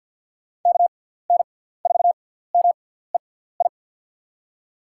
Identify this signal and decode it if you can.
Morse code, heard as a keyed tone, about 46 wpm, spaced as for 17 wpm, 700 Hz. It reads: KN4MEI
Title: USA Call Signs - 6 Characters - 6x Character Spacing - 50wpm